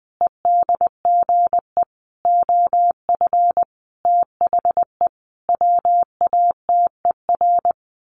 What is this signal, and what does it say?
Morse code, heard as a keyed tone, about 20 wpm, 700 Hz. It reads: EDGE OF THE WATER
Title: Sentences from Top 1300 Words 20wpm